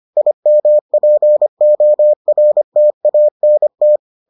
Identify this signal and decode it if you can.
Morse code, heard as a keyed tone, about 25 wpm, 600 Hz. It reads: IMPORTANT